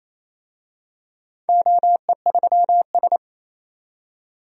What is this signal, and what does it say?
Morse code, heard as a keyed tone, about 28 wpm, 700 Hz. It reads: OE3S
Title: DXCC-100 Call Signs - 4 Characters - 28wpm